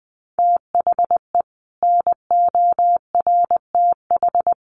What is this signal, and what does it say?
Morse code, heard as a keyed tone, about 20 wpm, 700 Hz. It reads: THE NORTH
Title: Sets of 5 Words Encore 20wpm